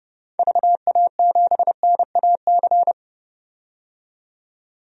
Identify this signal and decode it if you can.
Morse code, heard as a keyed tone, about 30 wpm, 700 Hz. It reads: VA7NAC
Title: DXCC-100 Call Signs - 6 Characters - 30wpm